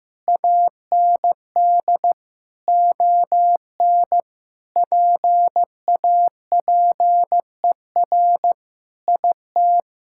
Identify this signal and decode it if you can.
Morse code, heard as a keyed tone, about 16 wpm, 700 Hz. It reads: AND ON PAPER IT
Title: Sentences from Top 1600 Words 15wpm